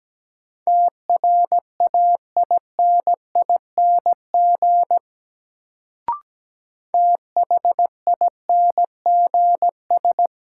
Morse code, heard as a keyed tone, about 17 wpm, 700 Hz. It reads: TRAINING THINGS
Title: Sentences from Top 1000 Words - Review Part 4 - New Word Drill - 17wpm